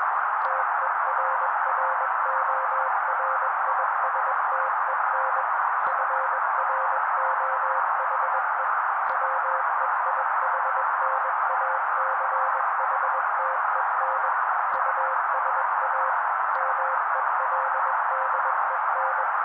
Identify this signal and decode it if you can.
Morse code, heard as a keyed tone, about 21 wpm, 530 Hz. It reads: TERRORISTEN FROHE WEIHNACHTEN USA MELDEN